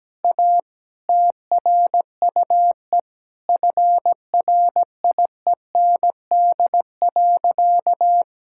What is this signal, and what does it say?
Morse code, heard as a keyed tone, about 17 wpm, 700 Hz. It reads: A TRUE FRIEND.
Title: Sentences from Top 1000 Words - Review Part 6 - 17wpm